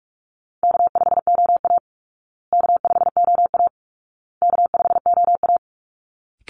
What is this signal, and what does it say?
Morse code, heard as a keyed tone, about 44 wpm, 700 Hz. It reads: K5OA K5OA K5OA
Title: Easy Call Signs Repeated 3x - 45wpm